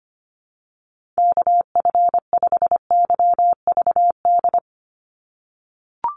Morse code, heard as a keyed tone, about 25 wpm, 700 Hz. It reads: KF5YVD
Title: USA Call Signs 25wpm